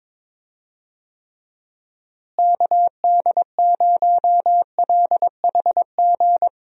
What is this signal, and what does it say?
Morse code, heard as a keyed tone, about 22 wpm, 700 Hz. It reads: KD0LHG
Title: USA Call Signs - 6 Characters - Repeated 2x - 22wpm